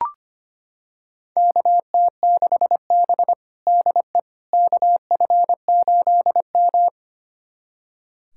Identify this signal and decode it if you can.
Morse code, heard as a keyed tone, about 25 wpm, 700 Hz. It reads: KT6B DE KF8M
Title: USA Callsign DE Callsign - 4 Characters - 25wpm